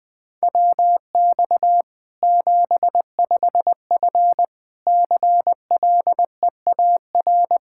Morse code, heard as a keyed tone, about 20 wpm, 700 Hz. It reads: WX 75F CLEAR
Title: Practice QSO - Exchange 2 - Info - 20wpm